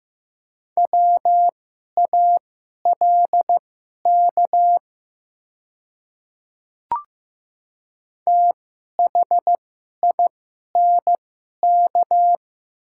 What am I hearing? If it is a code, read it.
Morse code, heard as a keyed tone, about 15 wpm, spaced as for 11 wpm, 690 Hz. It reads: WALK THINK
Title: 2x Character Spacing - Top 300 Words - 15wpm